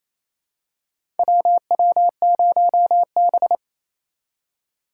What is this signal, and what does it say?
Morse code, heard as a keyed tone, about 28 wpm, 700 Hz. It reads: WW0B
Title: USA Call Signs - 4 Characters - 4x Character Spacing - 28wpm